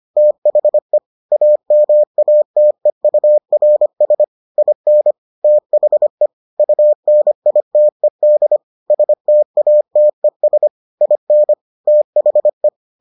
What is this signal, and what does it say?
Morse code, heard as a keyed tone, about 25 wpm, 600 Hz. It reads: THE AMATEURS IN THE UNITED STATES IN THE